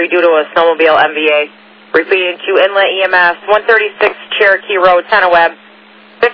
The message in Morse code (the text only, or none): none